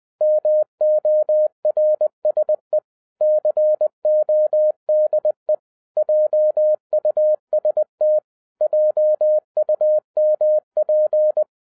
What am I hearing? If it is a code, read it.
Morse code, heard as a keyed tone, about 20 wpm, 610 Hz. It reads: MORSE CODE JUST JUMP